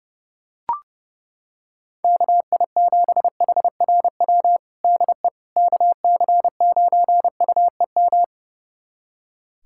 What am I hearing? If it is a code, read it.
Morse code, heard as a keyed tone, about 32 wpm, 700 Hz. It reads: KI7HRW DE KC9UEM
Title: USA Callsign DE Callsign - 6 Characters - 30wpm